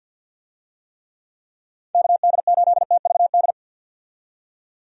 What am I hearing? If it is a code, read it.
Morse code, heard as a keyed tone, about 50 wpm, 700 Hz. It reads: KD8TVD
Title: USA Call Signs - 6 Characters - 50wpm